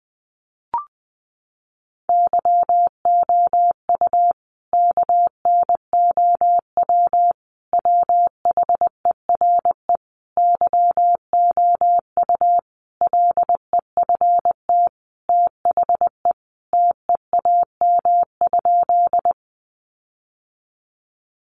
Morse code, heard as a keyed tone, about 19 wpm, 700 Hz. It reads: YOU KNOW WHERE YOU LEFT THE TEAM?
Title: Sentences from Top 1500 Words 20wpm